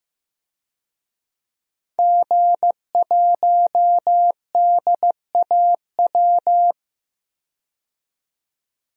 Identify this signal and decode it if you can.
Morse code, heard as a keyed tone, about 15 wpm, 700 Hz. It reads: G1DAW
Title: DXCC-100 Call Signs - 5 Characters - 15wpm